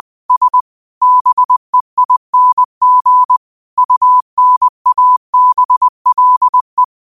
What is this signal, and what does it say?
Morse code, heard as a keyed tone, about 20 wpm, 1000 Hz. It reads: S BEING UNABLE
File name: qotdfiles / 20wpm